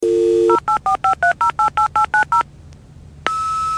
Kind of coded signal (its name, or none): DTMF